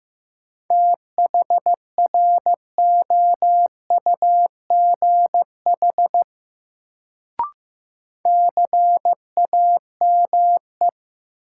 Morse code, heard as a keyed tone, about 15 wpm, 700 Hz. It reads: THROUGH CAME
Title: Sentences from Top 200 Words - New Word Drill - 15wpm